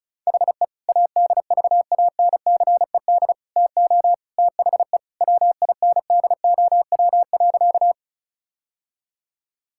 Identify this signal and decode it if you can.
Morse code, heard as a keyed tone, about 36 wpm, 700 Hz. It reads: HE ADVANCED TO THE WINDOW.
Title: Sentences from Top 1600 Words 35wpm